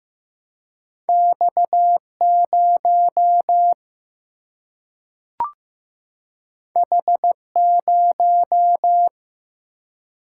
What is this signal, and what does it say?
Morse code, heard as a keyed tone, about 16 wpm, 700 Hz. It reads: X0 H0
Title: Single-Letter Single-Number 15wpm